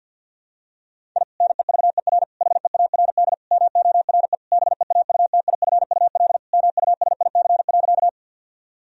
Morse code, heard as a keyed tone, about 52 wpm, 700 Hz. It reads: I NEVER HEARD MORE BEAUTIFUL MUSIC.
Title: Sentences from Top 1000 Words - Review Part 1 - 50wpm